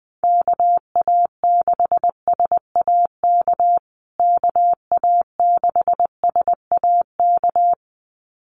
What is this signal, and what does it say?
Morse code, heard as a keyed tone, about 19 wpm, 700 Hz. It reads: KA6SAK KA6SAK
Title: USA Call Signs Repeated 2x 20wpm